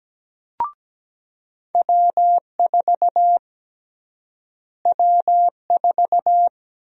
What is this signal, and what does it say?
Morse code, heard as a keyed tone, about 17 wpm, 700 Hz. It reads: W4 W4
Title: USA Call Sign Prefixes - Repeated 2x - 17wpm